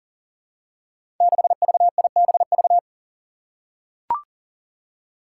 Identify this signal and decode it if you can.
Morse code, heard as a keyed tone, about 40 wpm, 700 Hz. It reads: 6VIBV
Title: Five Challenging Letter-Number Combinations 40wpm